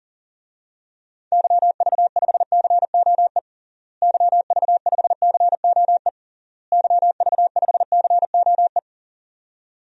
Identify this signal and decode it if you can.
Morse code, heard as a keyed tone, about 40 wpm, 700 Hz. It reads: YV5COE YV5COE YV5COE
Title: DXCC-100 Call Signs - 6 Characters - Repeated 3x - 40wpm